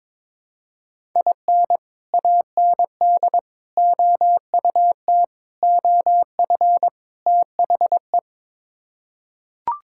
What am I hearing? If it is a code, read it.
Morse code, heard as a keyed tone, about 22 wpm, 700 Hz. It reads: IN AND OUT OF THE
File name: practice / 22wpm / Build A Phrase - 5-Word Phrases - 22wpm